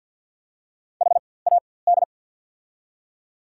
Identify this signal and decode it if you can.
Morse code, heard as a keyed tone, about 54 wpm, spaced as for 24 wpm, 690 Hz. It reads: HAD
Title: 4x Character Spacing - Top 100 Words - 50wpm